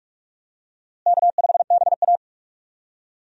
Morse code, heard as a keyed tone, about 46 wpm, 700 Hz. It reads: K5BA
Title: USA Call Signs - 4 Characters - Repeated 2x - 45wpm